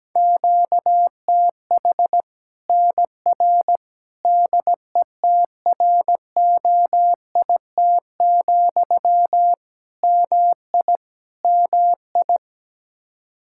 Morse code, heard as a keyed tone, about 17 wpm, 700 Hz. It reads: QTH NR DETROIT, MI MI
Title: Practice QSO - Exchange 1 - Beginning - 17wpm